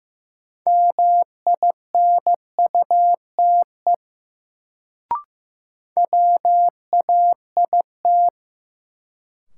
Morse code, heard as a keyed tone, about 15 wpm, 700 Hz. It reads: MINUTE WAIT
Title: Sentences from Top 500 Words - New Word Drill - 15wpm